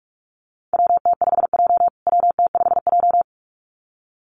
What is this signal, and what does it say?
Morse code, heard as a keyed tone, about 44 wpm, 700 Hz. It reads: WT5J WT5J
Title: USA Call Signs Repeated 2x 45wpm